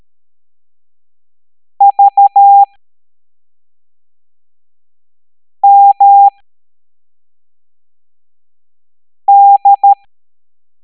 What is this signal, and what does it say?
Morse code, heard as a keyed tone, about 13 wpm, spaced as for 3 wpm, 800 Hz. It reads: VMD